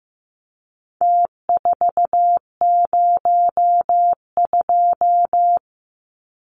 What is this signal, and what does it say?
Morse code, heard as a keyed tone, about 15 wpm, 700 Hz. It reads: T402
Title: Numbers - Contest Style - 4-Digit 15wpm